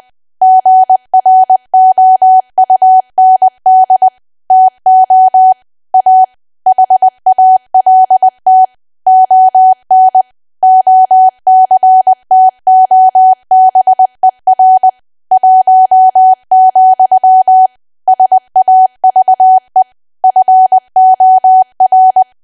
Morse code, heard as a keyed tone, about 20 wpm, 750 Hz. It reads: GROUND TO A HALT ON OCTOBER 1, SAVE FOR
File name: cwa / W1AW-Level3 / 20WPM